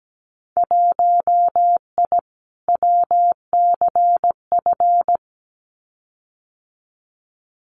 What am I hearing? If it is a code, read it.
Morse code, heard as a keyed tone, about 17 wpm, 700 Hz. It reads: 1I WCF
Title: Winter Field Day Exchanges Encore 17wpm